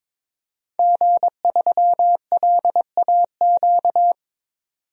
Morse code, heard as a keyed tone, about 22 wpm, 700 Hz. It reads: G3LAQ
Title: DXCC-100 Call Signs - 5 Characters - 22wpm